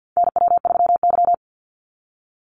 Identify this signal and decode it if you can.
Morse code, heard as a keyed tone, about 48 wpm, 700 Hz. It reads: NW3Y
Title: Call Signs - 50wpm